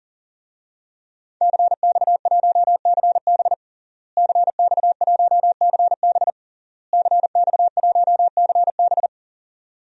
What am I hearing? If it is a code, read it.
Morse code, heard as a keyed tone, about 41 wpm, 700 Hz. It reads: CX1CB CX1CB CX1CB
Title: DXCC-100 Call Signs - 5 Characters - Repeated 3x - 40wpm